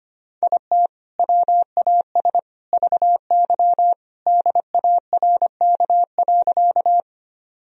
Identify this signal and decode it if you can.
Morse code, heard as a keyed tone, about 25 wpm, 700 Hz. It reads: IT WAS VY DARK.
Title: Sentences from Top 500 Words and Abbreviations 25wpm